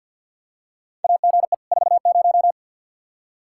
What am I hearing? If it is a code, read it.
Morse code, heard as a keyed tone, about 52 wpm, 700 Hz. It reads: AGE 40
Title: Practice QSO - Exchange 3 - More Info - 50wpm